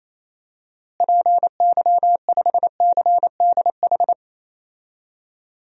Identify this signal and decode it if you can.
Morse code, heard as a keyed tone, about 28 wpm, 700 Hz. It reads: PY5CDH
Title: DXCC-100 Call Signs - 6 Characters - Repeated 3x - 28wpm